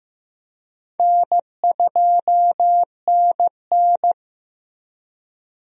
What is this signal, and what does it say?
Morse code, heard as a keyed tone, about 16 wpm, 700 Hz. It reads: N2NN
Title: USA Call Signs - 4 Characters - Repeated 3x - 15wpm